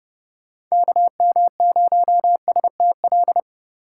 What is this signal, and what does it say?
Morse code, heard as a keyed tone, about 31 wpm, 700 Hz. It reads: KM0STL
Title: USA Call Signs - 6x Character Spacing - 30wpm